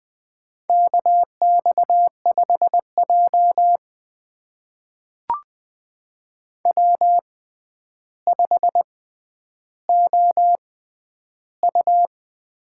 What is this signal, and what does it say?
Morse code, heard as a keyed tone, about 21 wpm, spaced as for 10 wpm, 700 Hz. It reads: KX5J W5OU
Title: USA Call Signs - 4 Characters - 6x Character Spacing - 20wpm